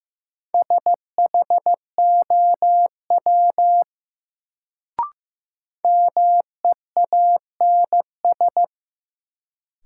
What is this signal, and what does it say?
Morse code, heard as a keyed tone, about 15 wpm, 700 Hz. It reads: SHOW MEANS
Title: Sets of 5 Words - New Word Drill - 15wpm